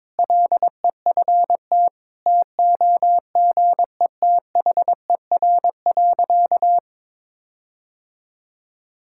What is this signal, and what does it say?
Morse code, heard as a keyed tone, about 22 wpm, 700 Hz. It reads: LEFT TOGETHER.
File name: practice / 22wpm / Sentences with 4-Word Phrases - 22wpm